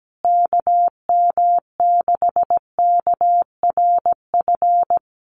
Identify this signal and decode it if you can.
Morse code, heard as a keyed tone, about 17 wpm, 700 Hz. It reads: KM6KRF
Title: USA Call Signs Repeated 3x 17wpm